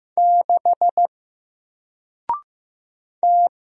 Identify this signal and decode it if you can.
Morse code, heard as a keyed tone, about 15 wpm, 700 Hz. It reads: 6 T
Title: Single Letter-Number Alt 15wpm